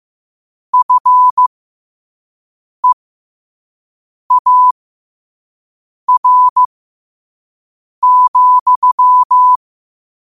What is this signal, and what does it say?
Morse code, heard as a keyed tone, about 15 wpm, spaced as for 5 wpm, 1000 Hz. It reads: FEAR,
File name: qotdfiles / 5wpm